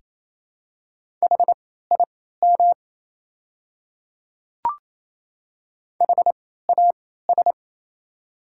Morse code, heard as a keyed tone, about 27 wpm, spaced as for 16 wpm, 690 Hz. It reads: HIM HAS
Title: 3x Character Spacing - Top 100 Words - 28wpm